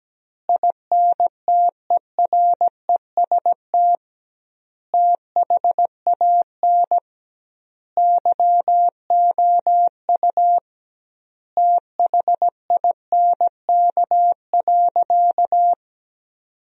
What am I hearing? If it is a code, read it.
Morse code, heard as a keyed tone, about 17 wpm, 700 Hz. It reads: INTEREST THAN YOU THINK.
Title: 2x Word-Spacing - Sentences from Top 400 Words - 17wpm